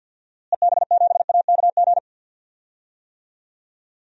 Morse code, heard as a keyed tone, about 52 wpm, 700 Hz. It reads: EB7AKG
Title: DXCC-100 Call Signs - 6 Characters - Repeated 3x - 50wpm